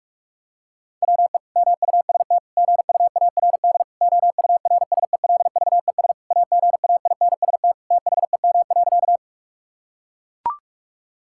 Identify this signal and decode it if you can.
Morse code, heard as a keyed tone, about 41 wpm, 700 Hz. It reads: WE MUST GUARD OURSELVES AGAINST THEM.
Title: Sentences from Top 1600 Words 45wpm